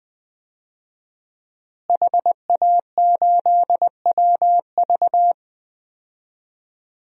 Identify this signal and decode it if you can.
Morse code, heard as a keyed tone, about 20 wpm, 700 Hz. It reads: HA8WV
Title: DXCC-100 Call Signs - 5 Characters - Repeated 2x - 20wpm